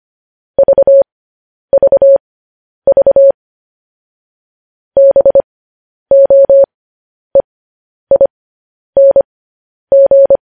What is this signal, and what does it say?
Morse code, heard as a keyed tone, about 25 wpm, spaced as for 10 wpm, 550 Hz. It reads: VVV BOEING
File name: podcast / News headlines-1x-25x10